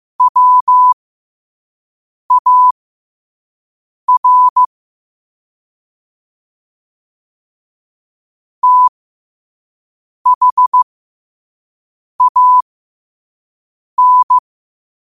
Morse code, heard as a keyed tone, about 15 wpm, spaced as for 5 wpm, 1000 Hz. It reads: WAR THAN